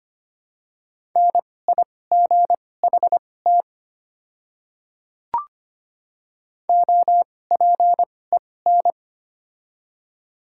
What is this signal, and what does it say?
Morse code, heard as a keyed tone, about 25 wpm, spaced as for 18 wpm, 700 Hz. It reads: NIGHT OPEN
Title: 2x Character Spacing - Top 300 Words - 25wpm